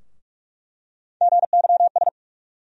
Morse code, heard as a keyed tone, about 42 wpm, 700 Hz. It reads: GYS
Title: USA Call Sign Postfixes - 3 Letters 45wpm